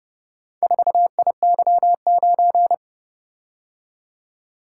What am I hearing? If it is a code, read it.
Morse code, heard as a keyed tone, about 30 wpm, 700 Hz. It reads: 4IY9